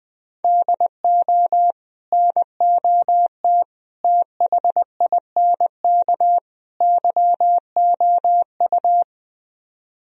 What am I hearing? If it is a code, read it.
Morse code, heard as a keyed tone, about 20 wpm, 700 Hz. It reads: DO NOT THINK YOU